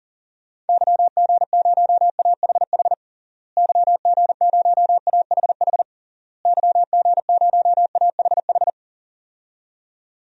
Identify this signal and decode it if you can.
Morse code, heard as a keyed tone, about 40 wpm, 700 Hz. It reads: YG0AHH YG0AHH YG0AHH
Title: DXCC-100 Call Signs - 6 Characters - Repeated 3x - 40wpm